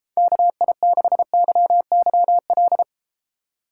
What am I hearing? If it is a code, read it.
Morse code, heard as a keyed tone, about 32 wpm, 700 Hz. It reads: KI6YYL